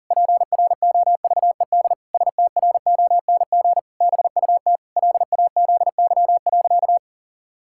Morse code, heard as a keyed tone, about 37 wpm, 700 Hz. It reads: PROVED STRONG BUT LAZY.
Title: Pangram Challenge v2 - 40wpm